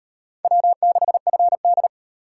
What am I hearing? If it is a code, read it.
Morse code, heard as a keyed tone, about 36 wpm, 700 Hz. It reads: W6FD